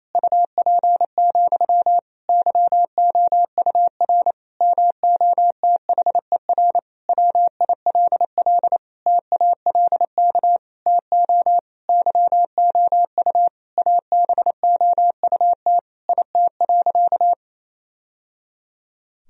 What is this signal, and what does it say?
Morse code, heard as a keyed tone, about 28 wpm, 700 Hz. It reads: UP, YOUR MOTHER WILL TALK TO YOU ABOUT IT.
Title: Sentences from Top 1000 Words - Review Part 1 - 28wpm